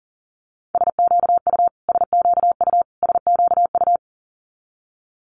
Morse code, heard as a keyed tone, about 40 wpm, 700 Hz. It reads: SQU SQU SQU
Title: Multi-letter Phonemes Repeated 3x 40wpm